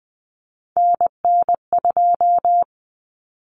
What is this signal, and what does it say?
Morse code, heard as a keyed tone, about 20 wpm, 700 Hz. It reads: NN2
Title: Common Cut Numbers - 3-Digit 20wpm